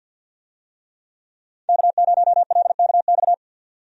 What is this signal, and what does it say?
Morse code, heard as a keyed tone, about 49 wpm, 700 Hz. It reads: K0LKX
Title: USA Call Signs - 5 Characters - 50wpm